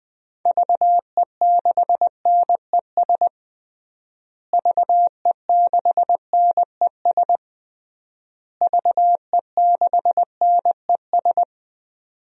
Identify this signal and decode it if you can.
Morse code, heard as a keyed tone, about 20 wpm, 700 Hz. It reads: VE6NES VE6NES VE6NES
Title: DXCC-100 Call Signs - 6 Characters - Repeated 3x - 20wpm